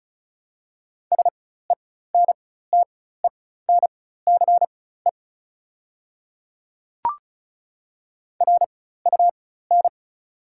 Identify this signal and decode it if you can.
Morse code, heard as a keyed tone, about 34 wpm, spaced as for 16 wpm, 690 Hz. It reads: SENTENCE RUN
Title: 4x Character Spacing - Top 300 Words - 35wpm